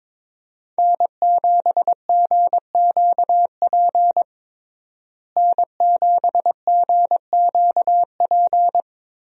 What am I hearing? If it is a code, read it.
Morse code, heard as a keyed tone, about 22 wpm, 700 Hz. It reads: N7GQP N7GQP
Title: USA Call Signs - 5 Characters - Repeated 3x - 22wpm